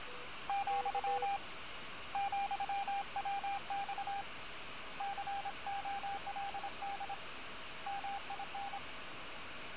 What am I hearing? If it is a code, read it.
Morse code, heard as a keyed tone, about 26 wpm, 800 Hz, with radio noise. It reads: , ,WX COLD MIN
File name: attachments / QSO HA9PP